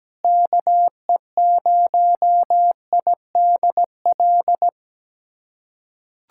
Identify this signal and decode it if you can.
Morse code, heard as a keyed tone, about 17 wpm, 700 Hz. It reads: KE0IDL
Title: USA Callsign DE Callsign - 6 Characters - 17wpm